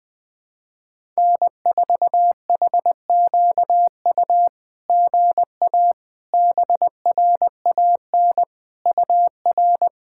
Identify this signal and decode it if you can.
Morse code, heard as a keyed tone, about 20 wpm, 700 Hz. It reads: N4HQU GA BRAN UR
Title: SOTA QSO Practice 20wpm